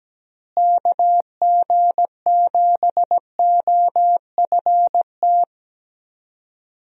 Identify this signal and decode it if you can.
Morse code, heard as a keyed tone, about 17 wpm, 700 Hz. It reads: KG7OFT
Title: USA Call Signs - 6 Characters - Repeated 3x - 17wpm